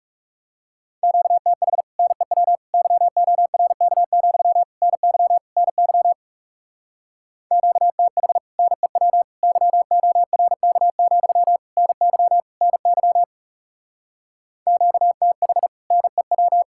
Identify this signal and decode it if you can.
Morse code, heard as a keyed tone, about 39 wpm, 700 Hz. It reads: QTH NEW YORK, NY NY QTH NEW YORK, NY NY QTH NEW